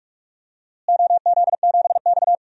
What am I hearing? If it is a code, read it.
Morse code, heard as a keyed tone, about 42 wpm, 690 Hz. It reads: OZ7X
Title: DXCC-100 Call Signs - 4 Characters - Repeated 3x - 45wpm